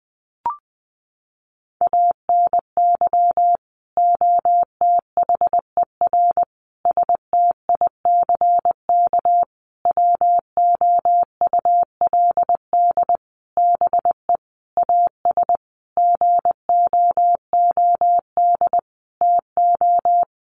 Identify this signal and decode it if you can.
Morse code, heard as a keyed tone, about 20 wpm, 700 Hz. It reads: ANY OTHER STICK WOULD BE AS GOOD TO
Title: Sentences from Top 1200 Words 20wpm